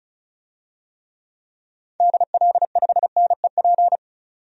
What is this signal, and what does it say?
Morse code, heard as a keyed tone, about 34 wpm, 700 Hz. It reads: DL5NEP
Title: DXCC-100 Call Signs - 6 Characters - 35wpm